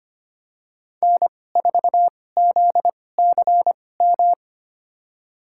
Morse code, heard as a keyed tone, about 25 wpm, spaced as for 18 wpm, 700 Hz. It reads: N4ZCM
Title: USA Call Signs - 5 Characters - 2x Character Spacing - 25wpm